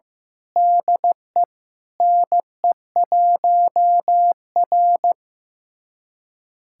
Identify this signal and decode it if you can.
Morse code, heard as a keyed tone, about 15 wpm, 700 Hz. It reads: DE NE1R
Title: USA Callsign DE Callsign - 4 Characters - 15wpm